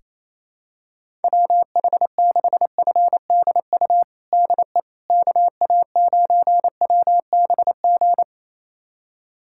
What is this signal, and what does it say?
Morse code, heard as a keyed tone, about 28 wpm, 700 Hz. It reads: WH6FDU DE KA9WBG
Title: USA Callsign DE Callsign - 6 Characters - 28wpm